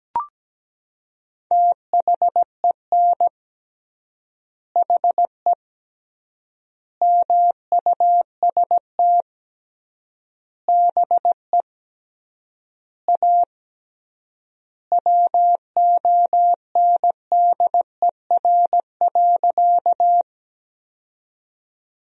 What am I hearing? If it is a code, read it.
Morse code, heard as a keyed tone, about 17 wpm, 700 Hz. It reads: THEN HE MUST BE A WONDER.
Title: 3x Word-Spacing - Sentences from Top 500 Words - 17wpm